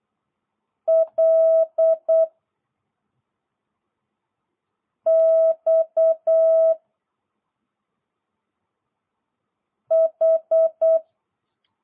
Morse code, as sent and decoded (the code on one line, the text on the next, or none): .-.. / -..- / ....
L X H